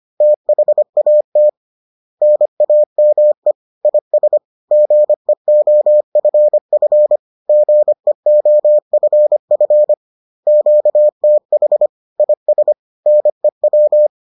Morse code, heard as a keyed tone, about 25 wpm, 600 Hz. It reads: THAT NAME IS GEOFF GEOFF QTH IS NEW